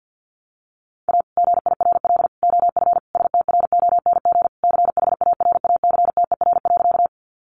Morse code, heard as a keyed tone, about 50 wpm, 700 Hz. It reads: A GIRL OF STRONG CHARACTER.
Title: Sentences from Top 1200 Words 50wpm